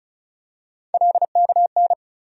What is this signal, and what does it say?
Morse code, heard as a keyed tone, about 36 wpm, 700 Hz. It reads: LKN
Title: USA Call Sign Postfixes - 3 Letters 35wpm